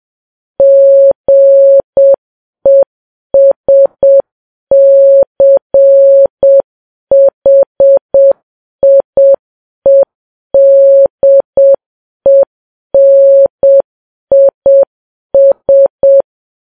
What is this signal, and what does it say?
Morse code, heard as a keyed tone, about 7 wpm, 550 Hz. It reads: GESCHIEDENIS
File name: Files Web tls / Morse on9fv / morse7kps